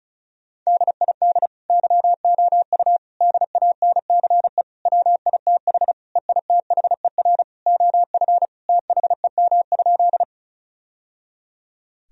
Morse code, heard as a keyed tone, about 36 wpm, 700 Hz. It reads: DID YOU DANCE WITH EITHER OF THEM?
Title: Sentences from Top 1000 Words - Review Part 4 - 35wpm